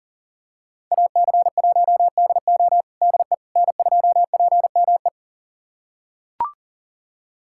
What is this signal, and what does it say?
Morse code, heard as a keyed tone, about 40 wpm, 700 Hz. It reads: AC1DO DE N2PME
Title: USA Callsign DE Callsign - 5 Characters - 40wpm